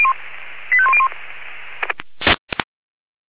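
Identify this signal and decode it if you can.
Five-tone selective call (ZVEI), heard as quick runs of five tones, tones of about 70 ms. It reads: 85101
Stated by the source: ZVEI five-tone selective call 85101 received by SDR.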